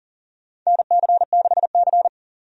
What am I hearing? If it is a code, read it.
Morse code, heard as a keyed tone, about 40 wpm, 700 Hz. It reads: NC6C
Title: USA Call Signs - 4 Characters - 40wpm